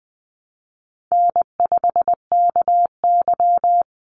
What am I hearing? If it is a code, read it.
Morse code, heard as a keyed tone, about 20 wpm, 700 Hz. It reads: N5KY
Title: Call Signs - 20wpm